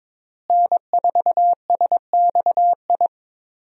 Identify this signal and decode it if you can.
Morse code, heard as a keyed tone, about 22 wpm, 700 Hz. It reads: N4SXI
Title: USA Call Signs - 5 Characters - Repeated 3x - 22wpm